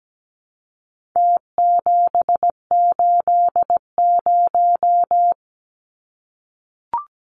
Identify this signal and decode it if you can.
Morse code, heard as a keyed tone, about 17 wpm, 700 Hz. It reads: T780
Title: Numbers - Contest Style - 4-Digit 17wpm